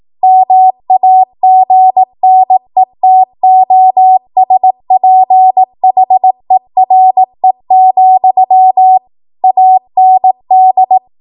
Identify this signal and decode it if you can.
Morse code, heard as a keyed tone, about 18 wpm, 750 Hz. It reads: MAGNETOSPHERE, AND
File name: Morse / 18WPM